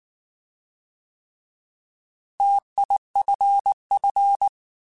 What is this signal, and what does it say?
Morse code, heard as a keyed tone, about 19 wpm, 800 Hz. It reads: TIFF
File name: Alphabetical / Alphabetical mp3s / Ts @18